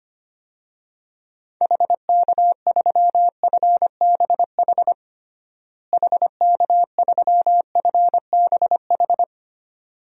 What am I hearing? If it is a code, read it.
Morse code, heard as a keyed tone, about 25 wpm, 700 Hz. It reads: HK3FBH HK3FBH
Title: DXCC-100 Call Signs - 6 Characters - Repeated 2x - 25wpm